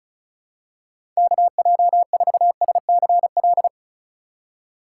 Morse code, heard as a keyed tone, about 34 wpm, 700 Hz. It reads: KJ4SCL
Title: USA Call Signs - 6 Characters - 2x Character Spacing - 35wpm